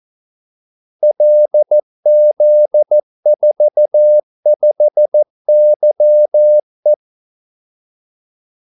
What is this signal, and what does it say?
Morse code, heard as a keyed tone, about 14 wpm, 600 Hz. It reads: LZ45YE
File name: Callsigns-14wpm